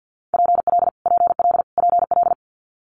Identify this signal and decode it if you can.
Morse code, heard as a keyed tone, about 50 wpm, 700 Hz. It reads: PL PL PL
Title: Multi-letter Phonemes Repeated 3x 50wpm